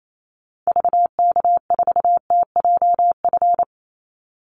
Morse code, heard as a keyed tone, about 28 wpm, 700 Hz. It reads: VK4TJF
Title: Call Signs - 28wpm